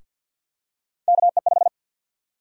Morse code, heard as a keyed tone, about 52 wpm, 700 Hz. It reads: KE5
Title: USA Call Sign Prefixes - Repeated 2x - 50wpm